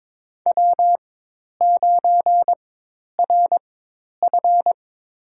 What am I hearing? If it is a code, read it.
Morse code, heard as a keyed tone, about 22 wpm, spaced as for 10 wpm, 700 Hz. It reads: W9RF
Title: USA Call Signs - 4 Characters - 4x Character Spacing - 22wpm